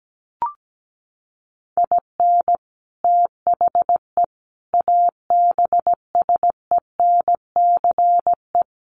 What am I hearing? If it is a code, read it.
Morse code, heard as a keyed tone, about 17 wpm, 700 Hz. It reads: IN THE ABSENCE
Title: Sets of 5 Words Encore 17wpm